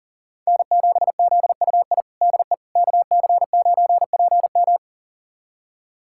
Morse code, heard as a keyed tone, about 40 wpm, 700 Hz. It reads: N7ZUI DE KC9PM
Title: USA Callsign DE Callsign - 5 Characters - 40wpm